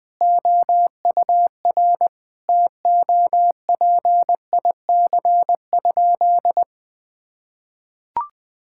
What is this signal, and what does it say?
Morse code, heard as a keyed tone, about 20 wpm, 700 Hz. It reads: OUR TOPIC?
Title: Sentences with 8-Word Phrases - 20wpm